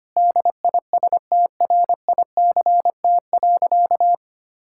Morse code, heard as a keyed tone, about 25 wpm, 700 Hz. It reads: DISTRICT.
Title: Sentences from Top 1600 Words 25wpm